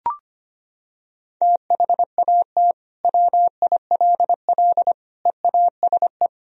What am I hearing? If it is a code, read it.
Morse code, heard as a keyed tone, about 25 wpm, 700 Hz. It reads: THAT WILL EASE
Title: Sentences from Top 1000 Words - Review Part 1 - 25wpm